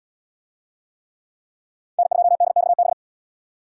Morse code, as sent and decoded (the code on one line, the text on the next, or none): -. ..--- -.. -..- -...
N2DXB